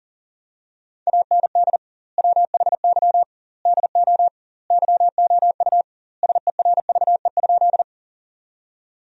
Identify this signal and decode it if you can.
Morse code, heard as a keyed tone, about 40 wpm, 700 Hz. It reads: AND WHY DO YOU SERVE?
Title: 2x Word-Spacing - Sentences from Top 400 Words - 40wpm